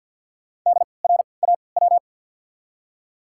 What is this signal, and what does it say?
Morse code, heard as a keyed tone, about 52 wpm, spaced as for 29 wpm, 700 Hz. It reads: DRAW
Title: 3x Character Spacing - Top 500 Words - 50wpm